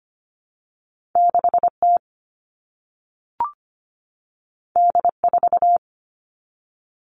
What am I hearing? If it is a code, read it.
Morse code, heard as a keyed tone, about 25 wpm, 700 Hz. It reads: 6T D4
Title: All Cut Numbers - 2-Digit 25wpm